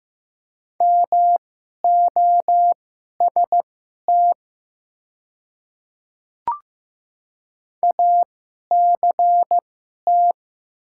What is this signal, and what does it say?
Morse code, heard as a keyed tone, about 15 wpm, spaced as for 11 wpm, 700 Hz. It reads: MOST ACT
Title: 2x Character Spacing - Top 200 Words - 15wpm